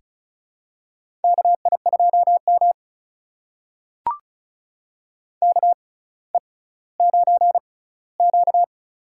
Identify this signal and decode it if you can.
Morse code, heard as a keyed tone, about 36 wpm, spaced as for 18 wpm, 700 Hz. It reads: KI2M KE9Q
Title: USA Call Signs - 4 Characters - 6x Character Spacing - 35wpm